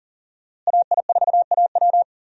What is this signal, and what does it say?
Morse code, heard as a keyed tone, about 40 wpm, 700 Hz. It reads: AI4AW
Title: USA Call Signs - 5 Characters - Repeated 3x - 40wpm